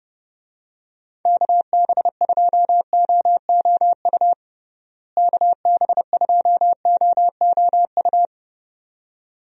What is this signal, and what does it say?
Morse code, heard as a keyed tone, about 31 wpm, 700 Hz. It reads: KB2OOU KB2OOU
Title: USA Call Signs - 6 Characters - Repeated 2x - 30wpm